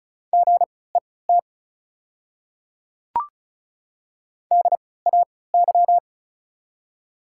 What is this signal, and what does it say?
Morse code, heard as a keyed tone, about 36 wpm, spaced as for 20 wpm, 700 Hz. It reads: GET DAY
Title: 3x Character Spacing - Top 200 Words - 35wpm